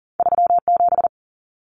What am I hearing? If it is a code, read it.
Morse code, heard as a keyed tone, about 40 wpm, 700 Hz. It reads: 37
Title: Numbers - 2-Digit 40wpm